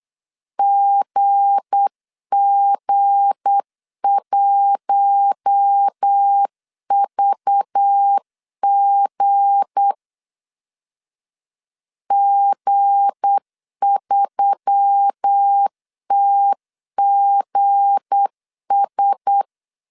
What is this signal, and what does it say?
Morse code, heard as a keyed tone, about 8 wpm, 800 Hz. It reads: GG1VG G3TGS
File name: LEZIONI / 7 GF9Z / 7 G 8wpm 3118